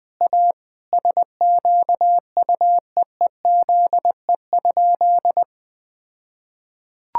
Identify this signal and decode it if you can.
Morse code, heard as a keyed tone, about 21 wpm, 700 Hz. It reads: A SQUEEZE?
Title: All-Age Appropriate Jokes - 20wpm